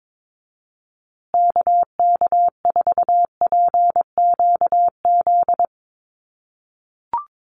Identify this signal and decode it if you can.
Morse code, heard as a keyed tone, about 22 wpm, 700 Hz. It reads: KK4PQZ
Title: USA Call Signs Repeated 2x 22wpm